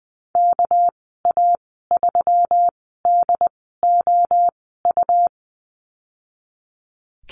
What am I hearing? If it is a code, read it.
Morse code, heard as a keyed tone, about 20 wpm, spaced as for 14 wpm, 700 Hz. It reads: KA3DOU
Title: 245 Lesson 39g - Letter z - Call Signs 20wpm